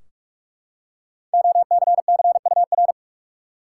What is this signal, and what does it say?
Morse code, heard as a keyed tone, about 41 wpm, 700 Hz. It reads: OCCUR
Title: Five Letter Words - 45wpm